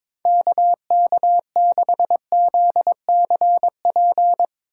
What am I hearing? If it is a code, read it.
Morse code, heard as a keyed tone, about 22 wpm, 700 Hz. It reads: KK6ZCP
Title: USA Call Signs - 3x Character Spacing - 22wpm